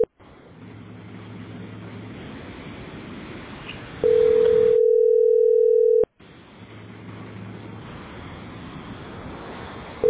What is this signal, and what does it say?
Ringback tone (North America), 440 + 480 Hz.